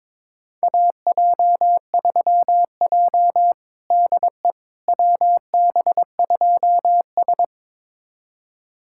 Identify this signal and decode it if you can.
Morse code, heard as a keyed tone, about 22 wpm, 700 Hz. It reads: AJ3J DE WB2S
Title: USA Callsign DE Callsign - 4 Characters - 22wpm